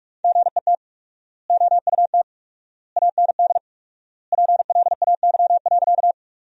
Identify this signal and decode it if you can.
Morse code, heard as a keyed tone, about 45 wpm, 700 Hz. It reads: GET OUT AND PLAY.
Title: 4x Word-Spacing - Sentences from Top 200 Words - 45wpm